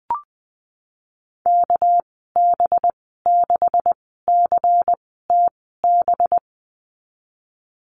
Morse code, heard as a keyed tone, about 20 wpm, spaced as for 15 wpm, 700 Hz. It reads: KB6CTB
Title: 207 Lesson 33g - Letter j - Call Signs 20wpm